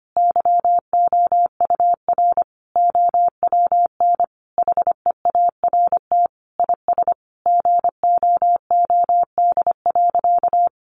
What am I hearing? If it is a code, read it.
Morse code, heard as a keyed tone, about 25 wpm, 700 Hz. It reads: YOUR OWN HEART IS GOOD.
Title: Sentences from Top 1200 Words 25wpm